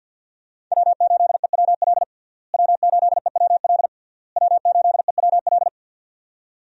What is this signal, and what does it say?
Morse code, heard as a keyed tone, about 46 wpm, 700 Hz. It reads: W8EWL W8EWL W8EWL
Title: USA Call Signs - 5 Characters - Repeated 3x - 50wpm